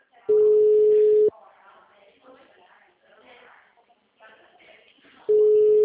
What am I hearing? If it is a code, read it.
Ringback tone (Europe, 425 Hz), with speech.